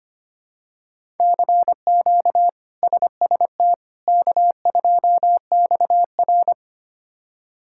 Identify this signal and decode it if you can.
Morse code, heard as a keyed tone, about 25 wpm, 700 Hz. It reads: CQ SST K2XR
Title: K1USN SST QSO Practice - 25wpm